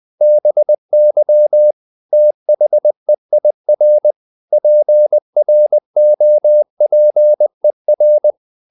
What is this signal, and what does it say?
Morse code, heard as a keyed tone, about 20 wpm, 600 Hz. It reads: BY THEIR PROPER
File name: the old man and the sea / the old man and the sea part 1 20wpm